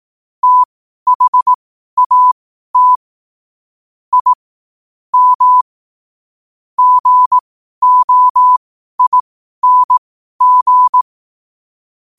Morse code, heard as a keyed tone, about 18 wpm, spaced as for 12 wpm, 1000 Hz. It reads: THAT I M GOING
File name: qotdfiles / 12wpm